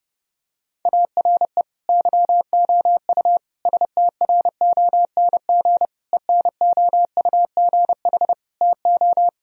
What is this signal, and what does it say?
Morse code, heard as a keyed tone, about 32 wpm, 700 Hz. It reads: ARE YOU STRONG ENOUGH TO
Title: Sentences from Top 1000 Words - Review Part 3 - 30wpm